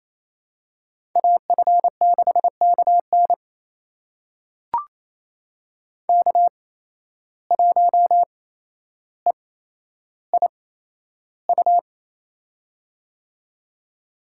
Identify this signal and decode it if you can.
Morse code, heard as a keyed tone, about 28 wpm, spaced as for 12 wpm, 700 Hz. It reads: AF6KN K1EIU
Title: USA Call Signs - 5 Characters - 8x Character Spacing - 28wpm